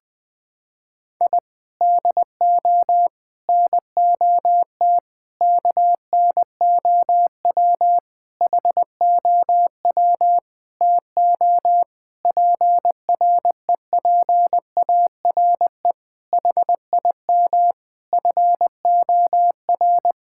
Morse code, heard as a keyed tone, about 19 wpm, 700 Hz. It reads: I DO NOT KNOW HOW TO PREPARE HIM FOR
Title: Sentences from Top 1000 Words - Review Part 6 - 20wpm